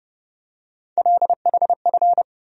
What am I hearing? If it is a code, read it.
Morse code, heard as a keyed tone, about 30 wpm, 700 Hz. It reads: LHF
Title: USA Call Sign Postfixes - 3 Letters 30wpm